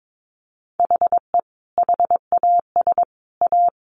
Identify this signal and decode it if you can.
Morse code, heard as a keyed tone, about 22 wpm, 700 Hz. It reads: HE HAS A
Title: Sentences from Top 1400 Words 22wpm